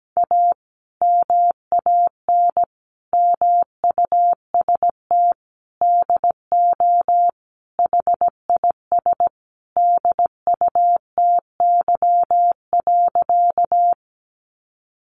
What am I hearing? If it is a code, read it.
Morse code, heard as a keyed tone, about 17 wpm, 700 Hz. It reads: A MAN MUST DO HIS DUTY.
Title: Sentences from Top 1500 Words 17wpm